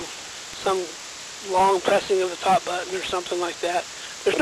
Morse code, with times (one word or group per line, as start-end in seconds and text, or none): none